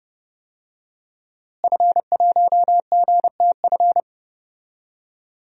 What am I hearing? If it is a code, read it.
Morse code, heard as a keyed tone, about 30 wpm, 700 Hz. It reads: F1GTF
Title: DXCC-100 Call Signs - 5 Characters - Repeated 2x - 30wpm